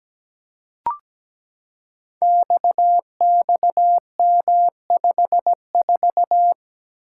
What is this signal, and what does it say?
Morse code, heard as a keyed tone, about 17 wpm, 700 Hz. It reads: XXM54
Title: Code Groups - 5 Letter:Numbers - 17wpm